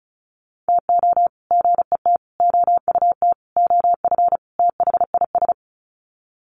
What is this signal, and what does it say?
Morse code, heard as a keyed tone, about 34 wpm, 700 Hz. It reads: TO GET OUT OF THIS
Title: Sets of 5 Words Encore2 35wpm